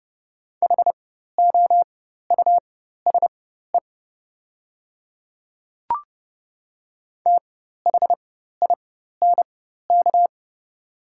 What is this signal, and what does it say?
Morse code, heard as a keyed tone, about 30 wpm, spaced as for 14 wpm, 700 Hz. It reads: HOUSE THINK
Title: 4x Character Spacing - Top 300 Words - 30wpm